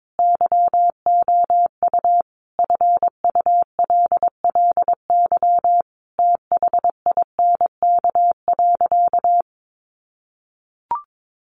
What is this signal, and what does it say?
Morse code, heard as a keyed tone, about 22 wpm, 700 Hz. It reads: YOU FULLY THINK.
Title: Sentences from Top 1300 Words 22wpm